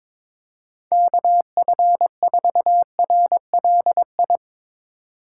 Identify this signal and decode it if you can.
Morse code, heard as a keyed tone, about 22 wpm, 700 Hz. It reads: KF4RLI
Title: Speed Racing - USA Call Signs 22wpm